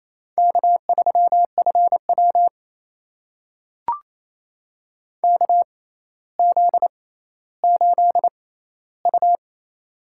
Morse code, heard as a keyed tone, about 28 wpm, spaced as for 14 wpm, 700 Hz. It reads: K3FW KZ8U
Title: USA Call Signs - 4 Characters - 6x Character Spacing - 28wpm